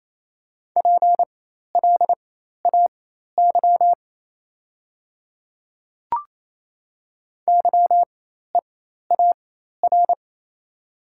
Morse code, heard as a keyed tone, about 28 wpm, spaced as for 13 wpm, 690 Hz. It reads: PLAY YEAR
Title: 4x Character Spacing - Top 200 Words - 28wpm